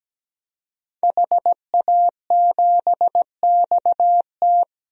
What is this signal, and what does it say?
Morse code, heard as a keyed tone, about 17 wpm, 700 Hz. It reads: HA7XT
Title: DXCC-100 Call Signs - 5 Characters - Repeated 2x - 17wpm